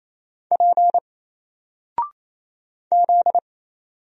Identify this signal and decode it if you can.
Morse code, heard as a keyed tone, about 27 wpm, 700 Hz. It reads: P Z
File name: Single-Letter-Number-28wpm